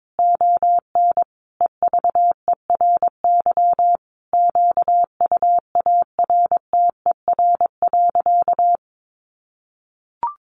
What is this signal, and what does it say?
Morse code, heard as a keyed tone, about 22 wpm, 700 Hz. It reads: ON EVERY QUARTER.
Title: Sentences from Top 1500 Words 22wpm